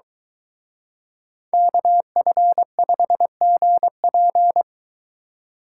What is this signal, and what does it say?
Morse code, heard as a keyed tone, about 23 wpm, 700 Hz. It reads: KF5GP